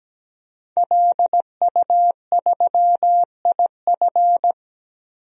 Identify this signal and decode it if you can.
Morse code, heard as a keyed tone, about 17 wpm, 700 Hz. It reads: LU3IF
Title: DXCC-100 Call Signs - 5 Characters - 17wpm